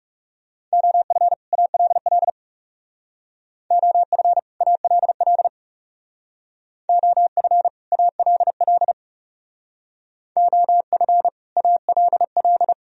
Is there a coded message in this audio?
Yes: Morse code from 0.7 to 12.7 s, heard as a keyed tone, about 38 wpm, 700 Hz. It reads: OF ALL OF ALL OF ALL OF ALL